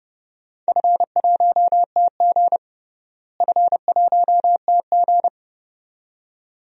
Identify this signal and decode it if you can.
Morse code, heard as a keyed tone, about 32 wpm, 700 Hz. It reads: F1TG F1TG
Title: DXCC-100 Call Signs - 4 Characters - Repeated 2x - 30wpm